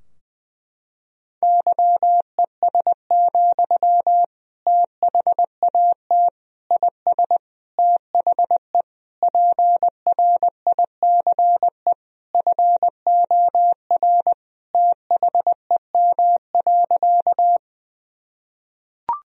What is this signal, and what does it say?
Morse code, heard as a keyed tone, about 20 wpm, 700 Hz. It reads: YES, THAT IS THE PRICE FOR THEM.
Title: Sentences from Top 1000 Words - Review Part 4 - 20wpm